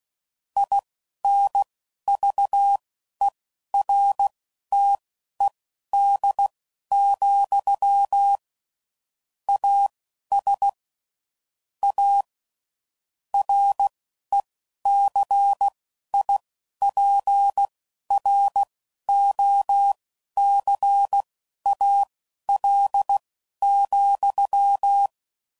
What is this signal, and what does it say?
Morse code, heard as a keyed tone, about 16 wpm, spaced as for 11 wpm, 800 Hz. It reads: INVERTED, AS A RECIPROCAL,
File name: CW Code Groups @10WPM / Technical Subjects / XC